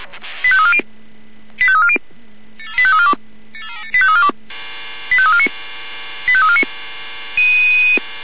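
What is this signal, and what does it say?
Five-tone selective call (ZVEI), heard as quick runs of five tones, tones of about 70 ms. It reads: 85340 85340 85342 85349 85349
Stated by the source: ZVEI five-tone call 85350 received by SDR.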